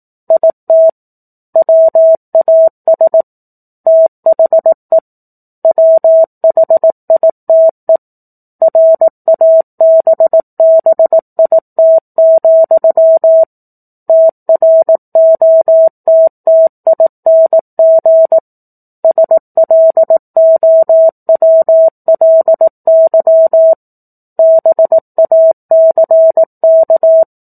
Morse code, heard as a keyed tone, about 18 wpm, 650 Hz. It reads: IT WAS THE WHITE RABBIT, TROTTING SLOWLY BACK
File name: 04-Chapter-04 18 WPM